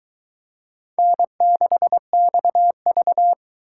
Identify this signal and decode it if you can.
Morse code, heard as a keyed tone, about 22 wpm, 700 Hz. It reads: N6XV